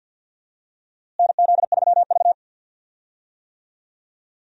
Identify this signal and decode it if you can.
Morse code, heard as a keyed tone, about 48 wpm, 700 Hz. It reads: NZ3V
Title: USA Call Signs - 4 Characters - Repeated 3x - 50wpm